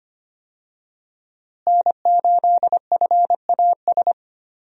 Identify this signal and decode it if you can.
Morse code, heard as a keyed tone, about 25 wpm, 700 Hz. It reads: N8FAS
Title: USA Call Signs - 5 Characters - Repeated 2x - 25wpm